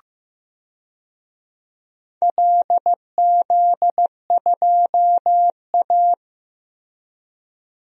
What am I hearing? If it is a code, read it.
Morse code, heard as a keyed tone, about 15 wpm, 700 Hz. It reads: LZ2A
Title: DXCC-100 Call Signs - 4 Characters - 15wpm